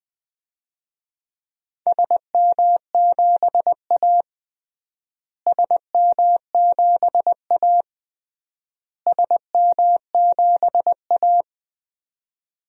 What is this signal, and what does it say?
Morse code, heard as a keyed tone, about 21 wpm, 700 Hz. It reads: SM7A SM7A SM7A
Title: DXCC-100 Call Signs - 4 Characters - Repeated 3x - 20wpm